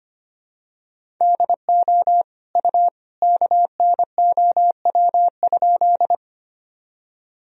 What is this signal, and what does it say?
Morse code, heard as a keyed tone, about 25 wpm, 700 Hz. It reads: DO U KNOW?
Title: Sentences from Top 100 Words and Abbreviations 25wpm